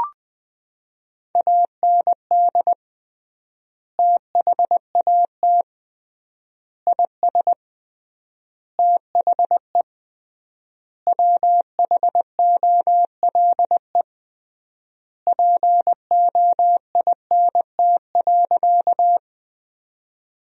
Morse code, heard as a keyed tone, about 20 wpm, 700 Hz. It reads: AND THAT IS THE WHOLE POINT.
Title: 3x Word-Spacing - Sentences from Top 400 Words - 20wpm